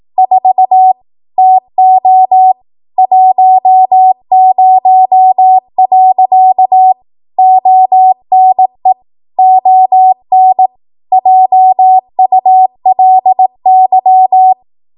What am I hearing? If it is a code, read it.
Morse code, heard as a keyed tone, about 18 wpm, 750 Hz. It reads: 4 TO 10. ONE ON JULY
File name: Archive / 18 WPM / 18WPM